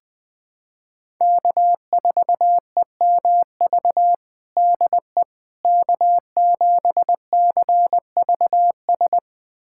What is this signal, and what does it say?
Morse code, heard as a keyed tone, about 20 wpm, 700 Hz. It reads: K4EMV DE K7CVS
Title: USA Callsign DE Callsign - 5 Characters - 20wpm